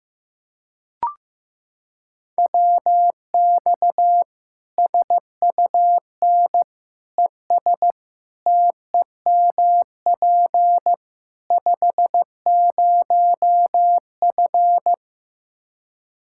Morse code, heard as a keyed tone, about 15 wpm, 700 Hz. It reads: WX SUN ES TEMP 50F
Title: Practice QSO - Exchange 2 - Info - 15wpm